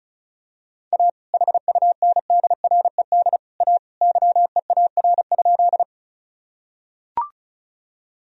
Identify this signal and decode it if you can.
Morse code, heard as a keyed tone, about 36 wpm, 700 Hz. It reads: A HUNDRED A YEAR?
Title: Sentences from Top 1000 Words - Review Part 2 - 35wpm